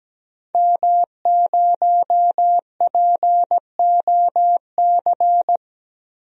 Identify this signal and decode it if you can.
Morse code, heard as a keyed tone, about 17 wpm, 700 Hz. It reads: M0POC
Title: DXCC-100 Call Signs - 5 Characters - 17wpm